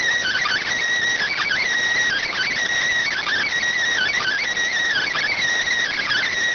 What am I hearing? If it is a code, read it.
Emergency Alert System (SAME) tones (header bursts), heard as harsh buzzing bursts. It is heard from 1.1 to 6.2 s.